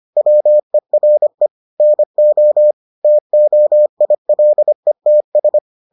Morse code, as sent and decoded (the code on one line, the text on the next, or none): .-- . .-. / -. --- / - --- .. .-.. - ...
WERE NO TOILETS